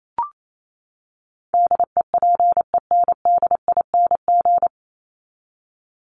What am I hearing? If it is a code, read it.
Morse code, heard as a keyed tone, about 28 wpm, 700 Hz. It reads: DEPENDING
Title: ING Suffix - Top 100 Words with Base Words 28wpm